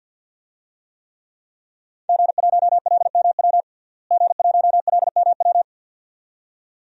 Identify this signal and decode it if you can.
Morse code, heard as a keyed tone, about 48 wpm, 700 Hz. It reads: G1LMW G1LMW
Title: DXCC-100 Call Signs - 5 Characters - Repeated 2x - 50wpm